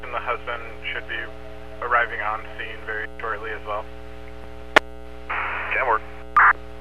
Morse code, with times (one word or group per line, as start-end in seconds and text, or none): none